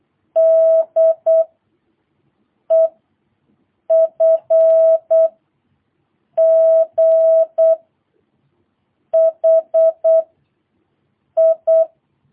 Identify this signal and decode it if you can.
Morse code, heard as a keyed tone, about 8 wpm, 650 Hz. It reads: D E F G H I
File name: jh8cbh / 2022cw